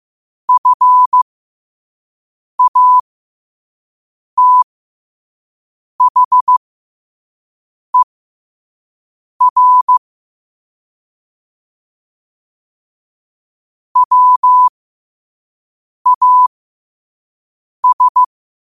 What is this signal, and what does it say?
Morse code, heard as a keyed tone, about 15 wpm, spaced as for 5 wpm, 1000 Hz. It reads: FATHER WAS
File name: qotdfiles / 5wpm